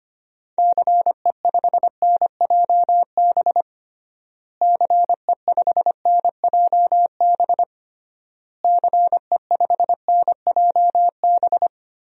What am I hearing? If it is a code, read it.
Morse code, heard as a keyed tone, about 25 wpm, 700 Hz. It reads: CE5NJB CE5NJB CE5NJB
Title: DXCC-100 Call Signs - 6 Characters - Repeated 3x - 25wpm